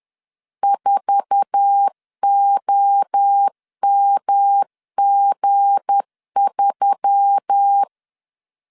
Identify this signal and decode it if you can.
Morse code, heard as a keyed tone, about 10 wpm, 800 Hz. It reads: 4OMG3
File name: LEZIONI / 10 SIGNS / 10D COMA 10wpm 3013